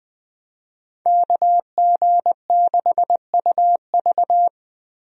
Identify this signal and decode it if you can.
Morse code, heard as a keyed tone, about 20 wpm, 700 Hz. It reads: KG6UV
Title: USA Call Signs - 5 Characters - 20wpm